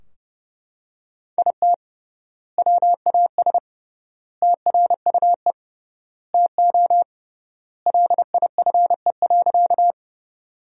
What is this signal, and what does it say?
Morse code, heard as a keyed tone, about 30 wpm, 700 Hz. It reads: IT WAS TRUE TO LIFE.
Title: 3x Word-Spacing - Sentences from Top 400 Words - 30wpm